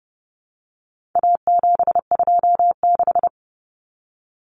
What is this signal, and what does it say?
Morse code, heard as a keyed tone, about 29 wpm, 700 Hz. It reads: A726
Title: Common Cut Numbers - 4-Digit 30wpm